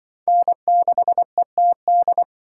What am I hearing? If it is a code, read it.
Morse code, heard as a keyed tone, about 23 wpm, 700 Hz. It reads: N6ETD
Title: Speed Racing - USA 1x3 Call Signs 20wpm